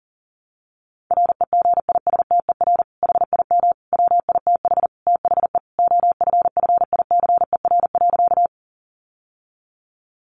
Morse code, heard as a keyed tone, about 40 wpm, 700 Hz. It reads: REGISTER HIM WITH THE OFFICER.
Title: Sentences from Top 1200 Words 40wpm